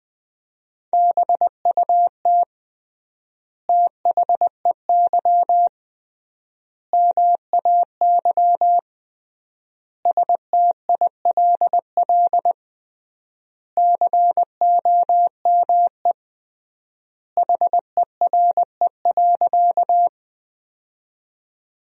Morse code, heard as a keyed tone, about 20 wpm, 700 Hz. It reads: BUT THEY MAY STILL COME HERE.
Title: 3x Word-Spacing - Sentences from Top 300 Words - 20wpm